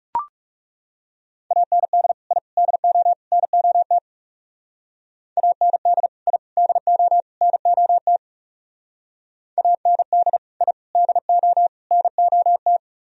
Morse code, heard as a keyed tone, about 39 wpm, 700 Hz. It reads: AND I DO NOT AND I DO NOT AND I DO NOT